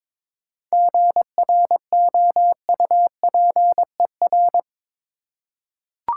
Morse code, heard as a keyed tone, about 22 wpm, 700 Hz. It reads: GROUPER
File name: Difficult Words with Hints - 22wpm